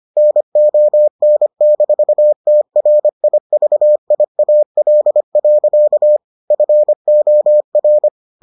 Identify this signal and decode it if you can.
Morse code, heard as a keyed tone, about 25 wpm, 600 Hz. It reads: NON-TRIVIAL. FOR